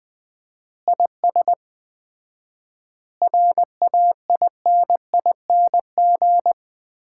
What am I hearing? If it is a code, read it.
Morse code, heard as a keyed tone, about 20 wpm, 700 Hz. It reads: IS RAINING
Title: Common QSO Phrases - 4x Word-Spacing - 20wpm